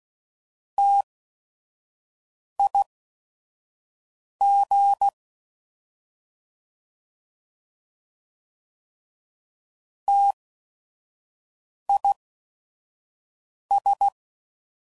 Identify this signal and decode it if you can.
Morse code, heard as a keyed tone, about 16 wpm, spaced as for 5 wpm, 800 Hz. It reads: TIG TIS
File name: Prefixes / By First Letter / Ts